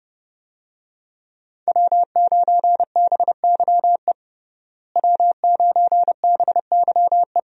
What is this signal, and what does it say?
Morse code, heard as a keyed tone, about 31 wpm, 700 Hz. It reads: W9BYE W9BYE
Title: USA Call Signs - 5 Characters - Repeated 3x - 30wpm